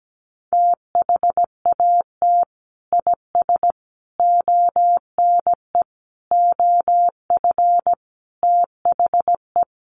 Morse code, heard as a keyed tone, about 17 wpm, 700 Hz. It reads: THAT IS ONE OF THE
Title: Sets of 5 Words Encore 17wpm